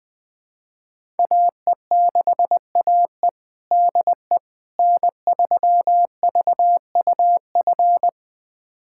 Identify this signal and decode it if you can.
Morse code, heard as a keyed tone, about 20 wpm, 700 Hz. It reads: AE6AE DE N3VUF
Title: USA Callsign DE Callsign - 5 Characters - 20wpm